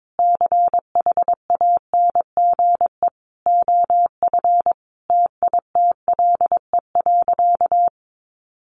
Morse code, heard as a keyed tone, about 22 wpm, 700 Hz. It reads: CHANGE OF TITLE.
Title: Sentences from Top 1200 Words 22wpm